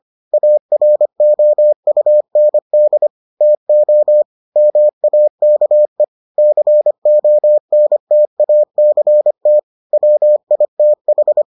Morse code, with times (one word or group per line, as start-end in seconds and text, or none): 0.3-3.1 AROUND
3.4-4.2 TO
4.6-6.0 MAKE
6.4-9.6 CONTACT
9.9-11.4 WITH